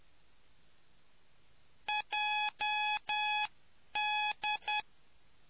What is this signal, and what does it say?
Morse code, heard as a keyed tone, about 10 wpm, spaced as for 9 wpm, 840 Hz. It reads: JD